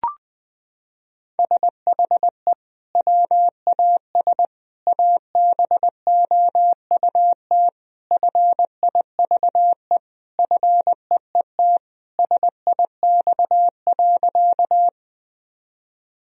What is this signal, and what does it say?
Morse code, heard as a keyed tone, about 20 wpm, 700 Hz. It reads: SHE WAS ABOUT FIVE FEET SIX.
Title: Sentences from Top 1000 Words - Review Part 2 - 20wpm